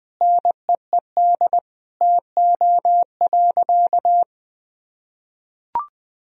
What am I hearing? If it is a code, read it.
Morse code, heard as a keyed tone, about 20 wpm, 700 Hz. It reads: NEED TO.
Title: Sentences from Top 1000 Words - Review Part 3 - 20wpm